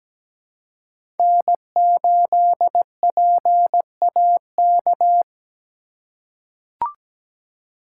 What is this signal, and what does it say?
Morse code, heard as a keyed tone, about 17 wpm, 700 Hz. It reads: N8PAK
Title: USA Call Signs - 4x Character Spacing - 17wpm